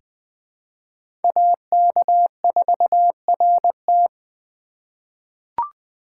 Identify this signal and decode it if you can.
Morse code, heard as a keyed tone, about 20 wpm, 700 Hz. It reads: AK4RT
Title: USA Call Signs - 5 Characters - 2x Character Spacing - 20wpm